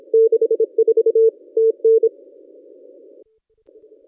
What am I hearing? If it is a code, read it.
Morse code, heard as a keyed tone, about 28 wpm, 440 Hz. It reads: 64 TN
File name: contests / ss-cw-k5tr-op-n5rz -1200